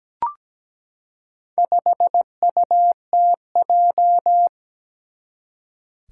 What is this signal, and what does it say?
Morse code, heard as a keyed tone, about 17 wpm, 700 Hz. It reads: 5UTJ
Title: Four Challenging Letter-Number Combinations 17wpm